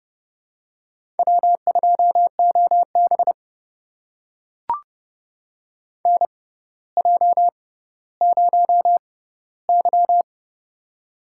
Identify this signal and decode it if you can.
Morse code, heard as a keyed tone, about 30 wpm, spaced as for 15 wpm, 700 Hz. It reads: W2OB NJ0Y
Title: USA Call Signs - 6x Character Spacing - 30wpm